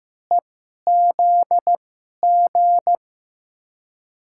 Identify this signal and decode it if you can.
Morse code, heard as a keyed tone, about 15 wpm, spaced as for 11 wpm, 700 Hz. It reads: EZG
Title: USA Call Signs - 6 Characters - 2x Character Spacing - 15wpm